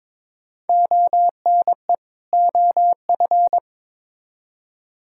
Morse code, heard as a keyed tone, about 22 wpm, 700 Hz. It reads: ONE OF
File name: Build A Phrase - 2-Word Phrases - 22wpm